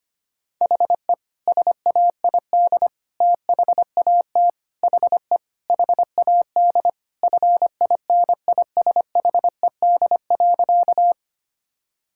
Morse code, heard as a keyed tone, about 25 wpm, 700 Hz. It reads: HE SAID THAT HE HAD FINISHED.
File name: practice / 25wpm / Sentences with 3-Word Phrases - 25wpm